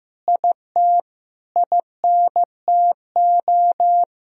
Morse code, heard as a keyed tone, about 15 wpm, 700 Hz. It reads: IT INTO
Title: Sentences from Top 1000 Words - Review Part 4 - 15wpm